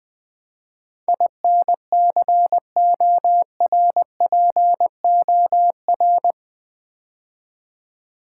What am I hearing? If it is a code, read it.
Morse code, heard as a keyed tone, about 20 wpm, 700 Hz. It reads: INCORPOR
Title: Build A Word - 4 Letters At A Time - 11-Letter Words - 20wpm